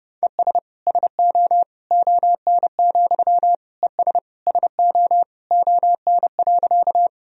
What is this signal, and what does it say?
Morse code, heard as a keyed tone, about 32 wpm, 700 Hz. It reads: ES SO ON, ES SO ON.
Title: Sentences from Top 100 Words and Abbreviations 30wpm